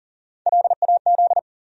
Morse code, heard as a keyed tone, about 40 wpm, 700 Hz. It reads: LAZ
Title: USA Call Sign Postfixes - 3 Letters 40wpm